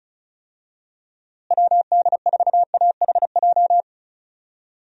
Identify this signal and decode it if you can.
Morse code, heard as a keyed tone, about 34 wpm, 700 Hz. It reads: WD4AHJ
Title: USA Call Signs - 6 Characters - Repeated 2x - 35wpm